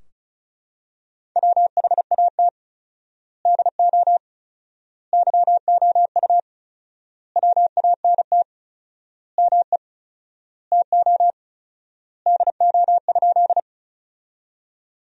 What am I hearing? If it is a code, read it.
Morse code, heard as a keyed tone, about 34 wpm, 700 Hz. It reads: WHAT DO YOU WANT ME TO DO?
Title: 4x Word-Spacing - Sentences from Top 200 Words - 35wpm